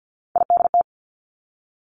Morse code, heard as a keyed tone, about 50 wpm, 700 Hz. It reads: IDT
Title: Three Challenging Letter-Number Combinations 50wpm